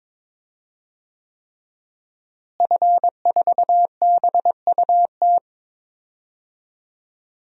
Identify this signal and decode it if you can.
Morse code, heard as a keyed tone, about 22 wpm, 700 Hz. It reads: F4BUT
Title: DXCC-100 Call Signs - 5 Characters - Repeated 3x - 22wpm